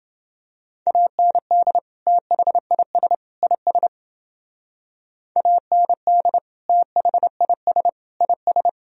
Morse code, heard as a keyed tone, about 30 wpm, 700 Hz. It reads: AND THIS IS AND THIS IS